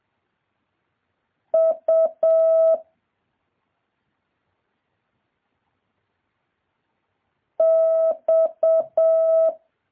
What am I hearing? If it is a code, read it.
Morse code, heard as a keyed tone, about 7 wpm, 650 Hz. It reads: U X
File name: jh8cbh / 2023cw